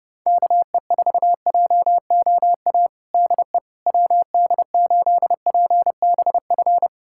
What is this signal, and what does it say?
Morse code, heard as a keyed tone, about 30 wpm, 700 Hz. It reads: KE4JOA DE WD8PBF
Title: USA Callsign DE Callsign - 6 Characters - 30wpm